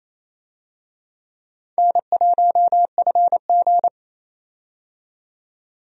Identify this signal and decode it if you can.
Morse code, heard as a keyed tone, about 28 wpm, 700 Hz. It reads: N1FG
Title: USA Call Signs - 4 Characters - 28wpm